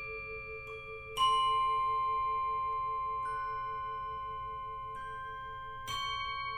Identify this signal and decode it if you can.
Emergency Alert System (SAME) tones (the 1050 Hz attention tone), with noise. It is heard from 1.1 to 5.9 s.